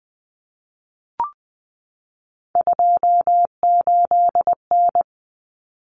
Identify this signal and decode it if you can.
Morse code, heard as a keyed tone, about 20 wpm, 700 Hz. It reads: 28N
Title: Common Cut Numbers - 3-Digit 20wpm